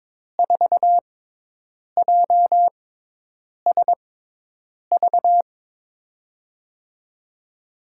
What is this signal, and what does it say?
Morse code, heard as a keyed tone, about 21 wpm, spaced as for 8 wpm, 700 Hz. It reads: 4JSV
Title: USA Call Signs - 6 Characters - 6x Character Spacing - 22wpm